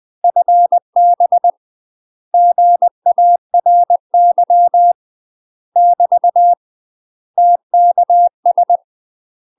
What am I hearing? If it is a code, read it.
Morse code, heard as a keyed tone, about 20 wpm, 700 Hz. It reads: FB GARY <BT> TKS
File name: UK MASTER QSOs TRAINING/20WPM